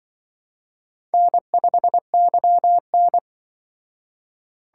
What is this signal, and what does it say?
Morse code, heard as a keyed tone, about 24 wpm, 700 Hz. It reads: N5YN
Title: Speed Racing - USA 1x2 Call Signs 20wpm